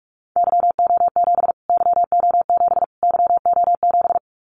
Morse code, heard as a keyed tone, about 44 wpm, 700 Hz. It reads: YO7 YO7 YO7
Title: Callsign Prefixes for Top 25 DXCC - Repeated 3x - 45wpm